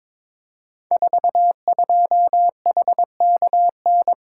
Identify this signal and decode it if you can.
Morse code, heard as a keyed tone, about 22 wpm, 700 Hz. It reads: 42HKN
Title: Code Groups - 5 Letter:Numbers - 22wpm